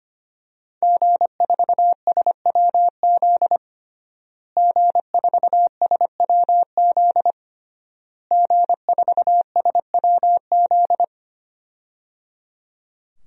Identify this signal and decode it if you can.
Morse code, heard as a keyed tone, about 25 wpm, 700 Hz. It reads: G4SWZ G4SWZ G4SWZ
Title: DXCC-100 Call Signs - 5 Characters - Repeated 3x - 25wpm